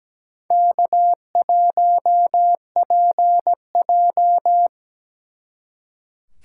Morse code, heard as a keyed tone, about 20 wpm, 700 Hz. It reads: K1PJ
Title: Speed Racing - USA 1x2 Call Signs 17wpm